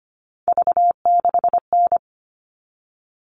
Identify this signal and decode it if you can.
Morse code, heard as a keyed tone, about 25 wpm, 700 Hz. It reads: V6N
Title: All Cut Numbers - 3-Digit 25wpm